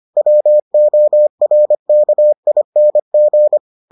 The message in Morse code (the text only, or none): WORKING